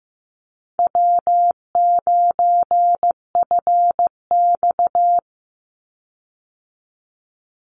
Morse code, heard as a keyed tone, about 15 wpm, 700 Hz. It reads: W9FX
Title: Call Signs Repeated 3x - 15wpm